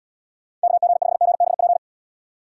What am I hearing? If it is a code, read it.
Morse code, heard as a keyed tone, about 78 wpm, 700 Hz. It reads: KG5GBP